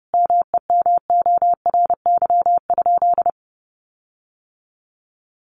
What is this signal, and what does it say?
Morse code, heard as a keyed tone, about 30 wpm, 700 Hz. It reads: MEMORY?
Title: Sentences from Top 1300 Words 30wpm